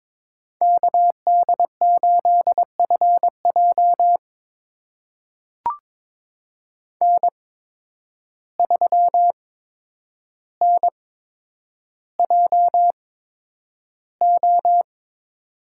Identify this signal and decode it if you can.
Morse code, heard as a keyed tone, about 22 wpm, spaced as for 9 wpm, 700 Hz. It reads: KD8FJ N3NJO
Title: USA Call Signs - 5 Characters - 8x Character Spacing - 22wpm